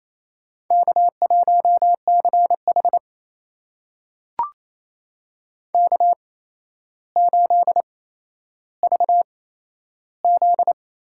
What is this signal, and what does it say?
Morse code, heard as a keyed tone, about 28 wpm, spaced as for 12 wpm, 700 Hz. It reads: K1CH K8VZ
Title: USA Call Signs - 4 Characters - 8x Character Spacing - 28wpm